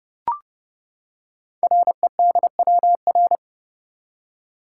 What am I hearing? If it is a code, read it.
Morse code, heard as a keyed tone, about 30 wpm, 690 Hz. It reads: REDWR
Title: Code Groups - 5 Letters - 30wpm